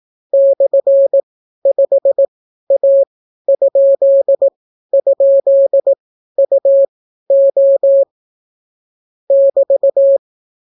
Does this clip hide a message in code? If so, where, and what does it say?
Yes: Morse code from 0.3 to 10.2 s, heard as a keyed tone, about 18 wpm, spaced as for 12 wpm, 550 Hz. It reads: /5A??UO <BT>